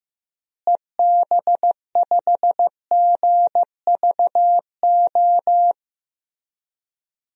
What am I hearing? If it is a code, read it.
Morse code, heard as a keyed tone, about 15 wpm, 700 Hz. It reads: EB5GVO
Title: DXCC-100 Call Signs - 6 Characters - Repeated 3x - 15wpm